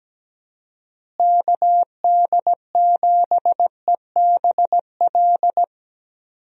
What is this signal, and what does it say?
Morse code, heard as a keyed tone, about 17 wpm, 700 Hz. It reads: KD7EBL
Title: USA Call Signs - 6 Characters - Repeated 3x - 17wpm